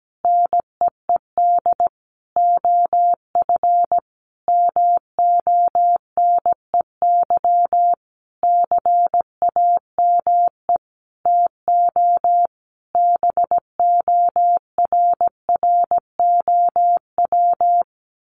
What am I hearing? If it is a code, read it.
Morse code, heard as a keyed tone, about 17 wpm, 700 Hz. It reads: NEED OF MONEY CAME TO BORROW
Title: Sentences from Top 1500 Words 17wpm